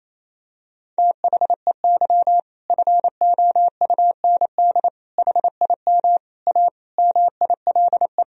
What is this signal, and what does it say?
Morse code, heard as a keyed tone, about 28 wpm, 700 Hz. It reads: THEY FOUND HIM A MILE
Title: Sentences from Top 1000 Words - Review Part 1 - 28wpm